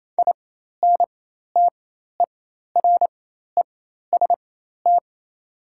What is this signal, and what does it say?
Morse code, heard as a keyed tone, about 28 wpm, spaced as for 13 wpm, 700 Hz. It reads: INTEREST
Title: 4x Character Spacing - Top 500 Words - 28wpm